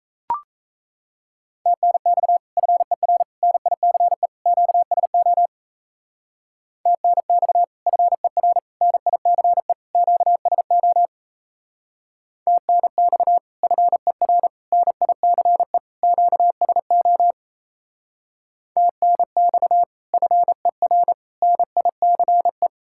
Morse code, heard as a keyed tone, about 36 wpm, 700 Hz. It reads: TNX FER NICE QSO TNX FER NICE QSO TNX FER NICE QSO TNX FER NICE